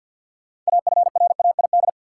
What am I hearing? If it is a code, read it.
Morse code, heard as a keyed tone, about 52 wpm, 700 Hz. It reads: AFRAID
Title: Sentences from Top 1500 Words - New Word Drill Alt - 50wpm